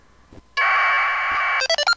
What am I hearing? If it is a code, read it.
Packet radio (AX.25, as in APRS).